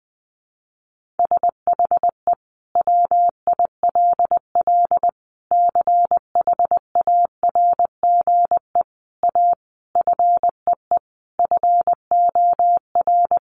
Morse code, heard as a keyed tone, about 20 wpm, 700 Hz. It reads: SHE WILL CHARGE A FEE FOR
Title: Sentences from Top 1500 Words 20wpm